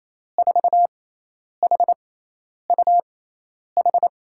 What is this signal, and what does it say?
Morse code, heard as a keyed tone, about 28 wpm, spaced as for 10 wpm, 700 Hz. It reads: 4HUH
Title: USA Call Signs - 6 Characters - 6x Character Spacing - 28wpm